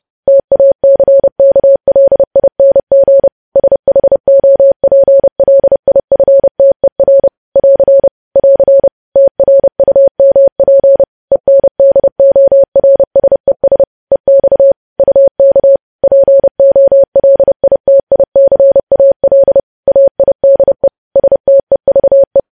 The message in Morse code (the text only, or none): TACKLING SHOPLIFTER <AR> <AR> TRUMP ENDORSES EX UK POLITICAL AIDE STEVE